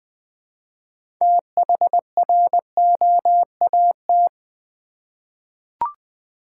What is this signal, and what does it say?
Morse code, heard as a keyed tone, about 20 wpm, 700 Hz. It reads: THROAT
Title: Six Letter Words - 20wpm